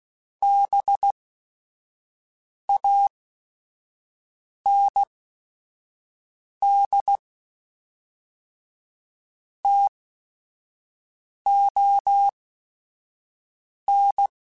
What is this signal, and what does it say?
Morse code, heard as a keyed tone, about 16 wpm, spaced as for 5 wpm, 800 Hz. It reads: BAND TON
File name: QSOs / QSOs-mp3s / WD0BGP 1@5